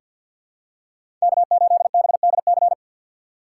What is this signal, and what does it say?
Morse code, heard as a keyed tone, about 49 wpm, 700 Hz. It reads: K8BDC
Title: USA Call Signs - 5 Characters - 3x Character Spacing - 50wpm